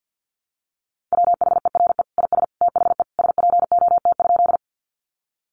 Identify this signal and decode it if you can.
Morse code, heard as a keyed tone, about 50 wpm, 700 Hz. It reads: WHERE IS THE SPOT?
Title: Sentences from Top 1100 Words 50wpm